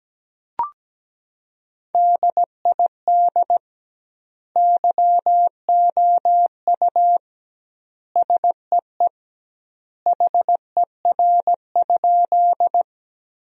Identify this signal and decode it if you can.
Morse code, heard as a keyed tone, about 17 wpm, 700 Hz. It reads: DID YOU SEE HER?
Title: 2x Word-Spacing - Sentences from Top 100 Words - 17wpm